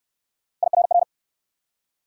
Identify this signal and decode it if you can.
Morse code, heard as a keyed tone, about 61 wpm, 700 Hz. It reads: IRR